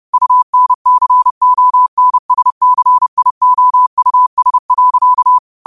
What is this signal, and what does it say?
Morse code, heard as a keyed tone, about 30 wpm, 1000 Hz. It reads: UNCONSCIOUS.
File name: 30wpm